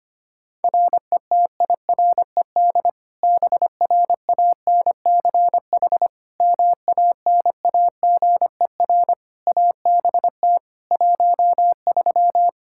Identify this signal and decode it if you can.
Morse code, heard as a keyed tone, about 25 wpm, 700 Hz. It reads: RETIRED BRANCH MANAGER ABT 13
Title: K1USN SST QSO Practice 25wpm